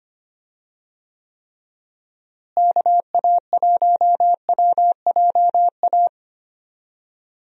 Morse code, heard as a keyed tone, about 25 wpm, 700 Hz. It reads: KA1WJA
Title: USA Call Signs - 6 Characters - Repeated 3x - 25wpm